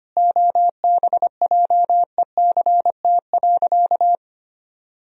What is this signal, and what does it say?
Morse code, heard as a keyed tone, about 25 wpm, 700 Hz. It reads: OBJECT.
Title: Sentences from Top 1000 Words - Review Part 3 - 25wpm